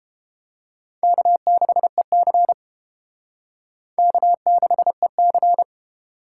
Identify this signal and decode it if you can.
Morse code, heard as a keyed tone, about 31 wpm, 700 Hz. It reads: K6EC K6EC